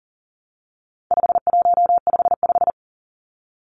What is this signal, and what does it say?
Morse code, heard as a keyed tone, about 40 wpm, 700 Hz. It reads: 5155
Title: Numbers - 4-Digit 40wpm